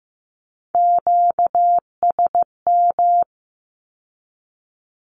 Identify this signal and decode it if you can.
Morse code, heard as a keyed tone, about 15 wpm, 700 Hz. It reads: QSM
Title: Q-Codes with Spoken Meaning - 15wpm